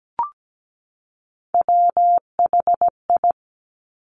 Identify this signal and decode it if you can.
Morse code, heard as a keyed tone, about 17 wpm, 700 Hz. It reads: WHI
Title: Top Three Letter Combinations in Words 17wpm